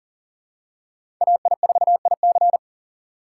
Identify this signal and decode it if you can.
Morse code, heard as a keyed tone, about 40 wpm, 700 Hz. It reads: AI4IC
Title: USA Call Signs - 5 Characters - 3x Character Spacing - 40wpm